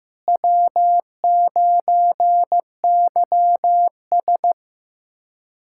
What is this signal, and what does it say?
Morse code, heard as a keyed tone, about 16 wpm, 700 Hz. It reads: W9YS
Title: K1USN SST QSO Practice - 15wpm